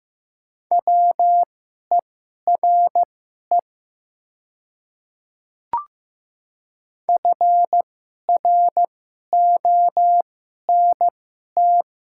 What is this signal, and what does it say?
Morse code, heard as a keyed tone, about 15 wpm, spaced as for 11 wpm, 700 Hz. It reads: WERE FRONT
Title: 2x Character Spacing - Top 500 Words - 15wpm